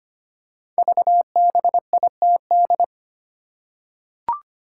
Morse code, heard as a keyed tone, about 25 wpm, 700 Hz. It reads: VBITD
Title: Code Groups - 5 Letters - 25wpm